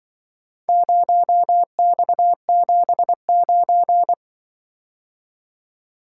Morse code, heard as a keyed tone, about 24 wpm, 700 Hz. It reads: 0X79
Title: Speed-Racing - Four Letter-Number Combinations - 20wpm